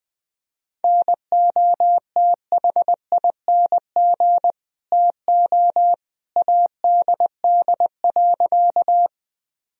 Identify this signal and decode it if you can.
Morse code, heard as a keyed tone, about 20 wpm, 700 Hz. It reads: NOTHING TO ADD.
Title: Sentences from Top 1000 Words - Review Part 1 - 20wpm